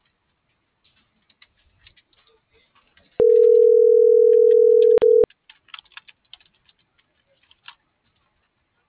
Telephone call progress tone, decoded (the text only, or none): ringback tone